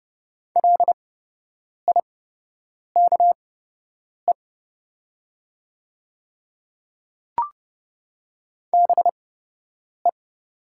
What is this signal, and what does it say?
Morse code, heard as a keyed tone, about 30 wpm, spaced as for 8 wpm, 700 Hz. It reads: LIKE BE
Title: 8x Character Spacing - Top 300 Words - 30wpm